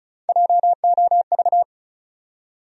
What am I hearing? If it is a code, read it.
Morse code, heard as a keyed tone, about 36 wpm, 700 Hz. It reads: JOV
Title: USA Call Sign Postfixes - 3 Letters 35wpm